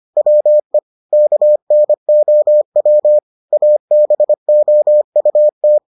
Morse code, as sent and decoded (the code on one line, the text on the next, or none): .-- . / -.- -. --- .-- / .- -... --- ..- -
WE KNOW ABOUT